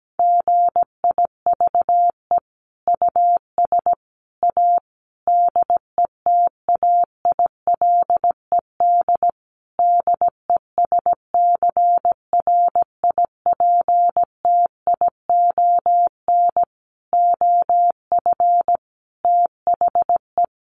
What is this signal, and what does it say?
Morse code, heard as a keyed tone, about 17 wpm, 700 Hz. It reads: GIVE US A DETAILED DESCRIPTION OF THE
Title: Sentences from Top 1500 Words 17wpm